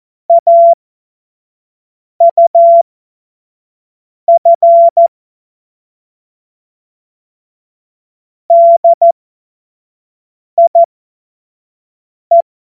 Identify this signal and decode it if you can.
Morse code, heard as a keyed tone, about 14 wpm, spaced as for 5 wpm, 680 Hz. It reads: AUF DIE
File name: Morsekurs / Witze / Witz 31.Oktober 2023 25 F70